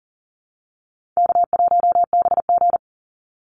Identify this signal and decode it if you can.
Morse code, heard as a keyed tone, about 40 wpm, 700 Hz. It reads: K1BG
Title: Easy Call Signs Repeated 3x - 40wpm